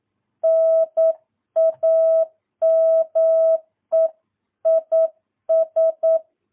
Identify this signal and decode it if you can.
Morse code, heard as a keyed tone, about 9 wpm, 650 Hz. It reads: NAME IS
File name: jh8cbh / 2022cw